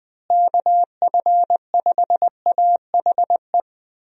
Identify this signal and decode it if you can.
Morse code, heard as a keyed tone, about 20 wpm, 700 Hz. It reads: KF5AHE
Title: USA Call Signs - 6 Characters - Repeated 2x - 20wpm